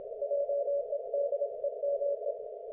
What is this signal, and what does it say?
Morse code, heard as a keyed tone, about 27 wpm, 570 Hz. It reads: JEDEME
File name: Archiv / CUC216 / ok1fhi cul216